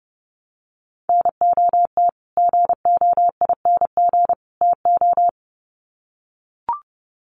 Morse code, heard as a keyed tone, about 30 wpm, 700 Hz. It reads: NOT GOING TO
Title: Sets of 3 Words 30wpm